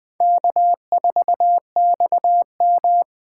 Morse code, heard as a keyed tone, about 21 wpm, 700 Hz. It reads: K4XM
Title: USA Call Signs - 4 Characters - 3x Character Spacing - 20wpm